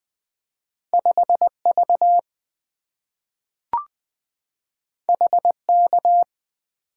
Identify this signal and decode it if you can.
Morse code, heard as a keyed tone, about 20 wpm, 690 Hz. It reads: 5V HK
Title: Two Letter-Number Combinations - 20wpm